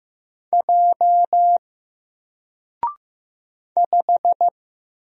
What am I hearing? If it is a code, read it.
Morse code, heard as a keyed tone, about 15 wpm, 700 Hz. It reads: J 5
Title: Single Letter-Number Alt 15wpm